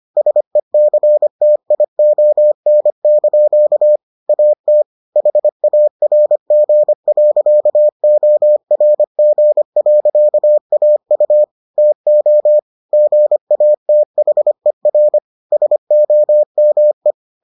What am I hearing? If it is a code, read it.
Morse code, heard as a keyed tone, about 25 wpm, 600 Hz. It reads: SECTION) AT HARG.ORG.AU TO GATHER SOME